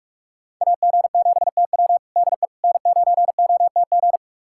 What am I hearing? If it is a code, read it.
Morse code, heard as a keyed tone, about 47 wpm, 700 Hz. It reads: AG7TW DE N9OTG
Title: USA Callsign DE Callsign - 5 Characters - 45wpm